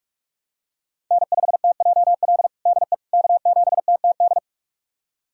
Morse code, heard as a keyed tone, about 46 wpm, 700 Hz. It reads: N5TJL DE K7TTD
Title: USA Callsign DE Callsign - 5 Characters - 45wpm